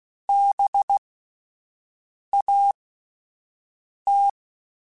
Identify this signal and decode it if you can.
Morse code, heard as a keyed tone, about 16 wpm, spaced as for 5 wpm, 800 Hz. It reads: BAT